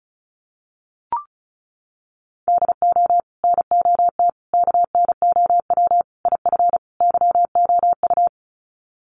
Morse code, heard as a keyed tone, about 34 wpm, 700 Hz. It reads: DO NOT KNOW IF YOU
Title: Sets of 5 Words 35wpm